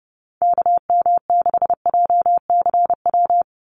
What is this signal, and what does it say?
Morse code, heard as a keyed tone, about 30 wpm, 700 Hz. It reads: KM6JCW
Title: USA Call Signs 30wpm